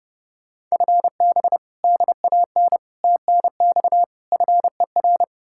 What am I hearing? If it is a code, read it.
Morse code, heard as a keyed tone, about 40 wpm, spaced as for 35 wpm, 700 Hz. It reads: FB DAN TNX FER
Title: Speed-Racing Common QSO Phrases - 30wpm